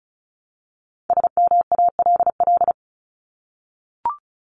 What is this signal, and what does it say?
Morse code, heard as a keyed tone, about 34 wpm, 700 Hz. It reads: SMALL
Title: Top 200 Words 35wpm